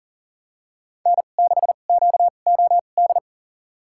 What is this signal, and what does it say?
Morse code, heard as a keyed tone, about 40 wpm, spaced as for 29 wpm, 700 Hz. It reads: N6QOD
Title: USA Call Signs - 5 Characters - 2x Character Spacing - 40wpm